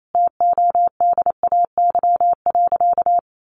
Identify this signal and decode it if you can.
Morse code, heard as a keyed tone, about 28 wpm, 700 Hz. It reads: TODAY.
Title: Sentences from Top 1500 Words 28wpm